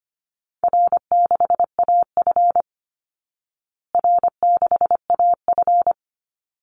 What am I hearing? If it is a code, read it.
Morse code, heard as a keyed tone, about 25 wpm, 700 Hz. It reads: R6AF R6AF
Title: Easy Call Signs Repeated 2x - 25wpm